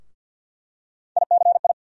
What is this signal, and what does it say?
Morse code, heard as a keyed tone, about 51 wpm, 700 Hz. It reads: ICI
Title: USA Call Sign Postfixes - 3 Letters 50wpm